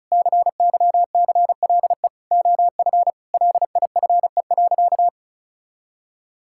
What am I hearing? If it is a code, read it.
Morse code, heard as a keyed tone, about 36 wpm, 700 Hz. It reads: CYCLE OF LIFE.
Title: Sentences from Top 1000 Words - Review Part 6 - 35wpm